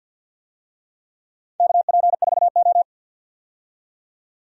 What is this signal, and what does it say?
Morse code, heard as a keyed tone, about 51 wpm, 700 Hz. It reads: KP4O
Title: DXCC-100 Call Signs - 4 Characters - 50wpm